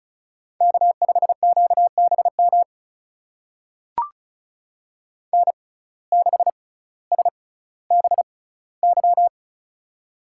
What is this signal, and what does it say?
Morse code, heard as a keyed tone, about 36 wpm, spaced as for 18 wpm, 700 Hz. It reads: K5QBM N6SBY
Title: USA Call Signs - 5 Characters - 6x Character Spacing - 35wpm